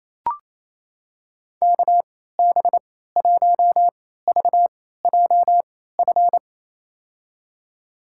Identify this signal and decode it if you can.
Morse code, heard as a keyed tone, about 28 wpm, spaced as for 16 wpm, 700 Hz. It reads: KB1VJF
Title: USA Call Signs - 6 Characters - 3x Character Spacing - 28wpm